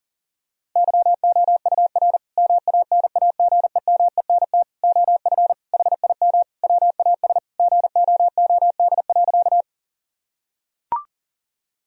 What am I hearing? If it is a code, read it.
Morse code, heard as a keyed tone, about 40 wpm, 700 Hz. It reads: YOUR MANAGEMENT OF HIM WAS GOOD.
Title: Sentences from Top 1000 Words - Review Part 4 - 40wpm